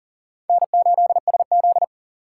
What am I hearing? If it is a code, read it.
Morse code, heard as a keyed tone, about 40 wpm, 700 Hz. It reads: N8SZ
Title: USA Call Signs - 4 Characters - 4x Character Spacing - 40wpm